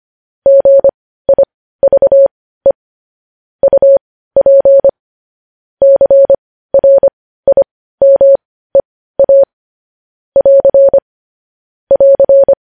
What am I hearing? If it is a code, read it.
Morse code, heard as a keyed tone, about 25 wpm, spaced as for 15 wpm, 550 Hz. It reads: GIVE UP CRIMEA <AR> <AR>
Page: News Headlines in Morse Code at 15 WPM